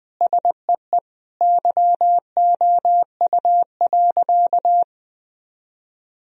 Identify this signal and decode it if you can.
Morse code, heard as a keyed tone, about 21 wpm, 700 Hz. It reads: SEE YOU.
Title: Sentences from Top 1000 Words - Review Part 1 - 20wpm